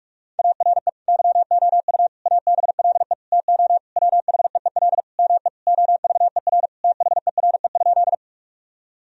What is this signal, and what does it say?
Morse code, heard as a keyed tone, about 41 wpm, 700 Hz. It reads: ARE YOU ABLE TO WHEEL ME OVER THERE?
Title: Sentences from Top 1000 Words - Review Part 3 - 45wpm